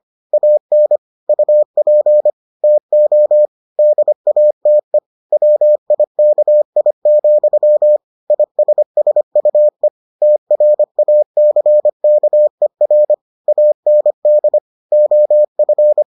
Morse code, heard as a keyed tone, about 25 wpm, 600 Hz. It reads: AN UP TO DATE WIKI, ISSUE TRACKER AND OF